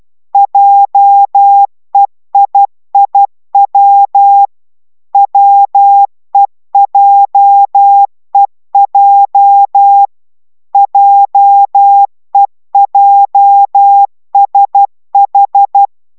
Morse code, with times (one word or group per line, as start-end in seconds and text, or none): 0.3-4.5 JEIIW
5.1-10.1 WEJEJ
10.7-15.9 JEJSH